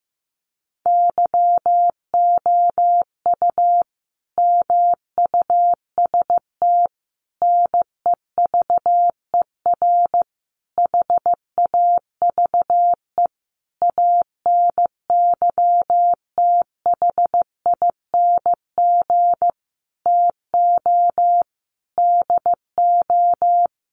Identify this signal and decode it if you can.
Morse code, heard as a keyed tone, about 15 wpm, 700 Hz. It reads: YOU MUST NEVER HAVE ANYTHING TO DO
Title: Sentences from Top 1000 Words 15wpm